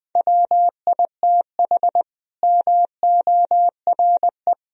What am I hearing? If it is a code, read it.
Morse code, heard as a keyed tone, about 20 wpm, 700 Hz. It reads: WITH MORE
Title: Sentences from Top 1000 Words - Review Part 3 - 20wpm